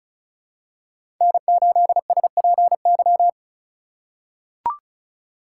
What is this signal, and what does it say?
Morse code, heard as a keyed tone, about 34 wpm, 700 Hz. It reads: N8SPY
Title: USA Call Signs - 5 Characters - 4x Character Spacing - 35wpm